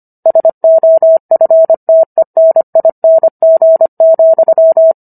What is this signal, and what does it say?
Morse code, heard as a keyed tone, about 25 wpm, 650 Hz. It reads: SOFTENING,